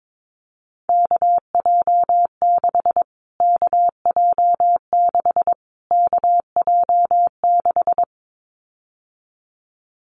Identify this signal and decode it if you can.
Morse code, heard as a keyed tone, about 22 wpm, 700 Hz. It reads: KJ6 KJ6 KJ6
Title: Callsign Prefixes for Top 25 DXCC - Repeated 3x - 22wpm